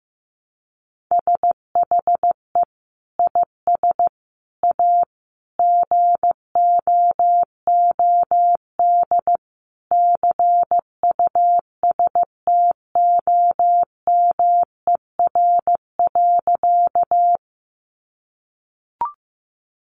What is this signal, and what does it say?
Morse code, heard as a keyed tone, about 15 wpm, 700 Hz. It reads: SHE IS A GOOD CUSTOMER.
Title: Sentences from Top 1100 Words 15wpm